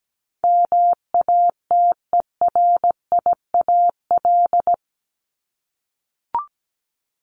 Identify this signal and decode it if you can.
Morse code, heard as a keyed tone, about 17 wpm, 700 Hz. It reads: MATERIAL
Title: ES:S Suffix - Top 100 Words with Base Words 17wpm